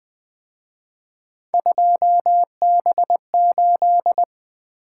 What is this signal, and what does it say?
Morse code, heard as a keyed tone, about 24 wpm, 700 Hz. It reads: 2B8
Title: Speed-Racing - Three Letter-Number Combinations - 20wpm